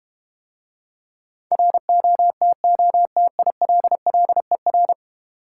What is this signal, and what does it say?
Morse code, heard as a keyed tone, about 32 wpm, 700 Hz. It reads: ROTOTILLER